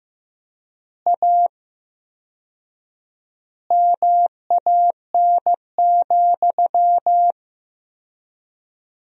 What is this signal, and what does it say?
Morse code, heard as a keyed tone, about 15 wpm, 700 Hz. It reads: A MAN,
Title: 4x Word-Spacing - Sentences from Top 200 Words - 15wpm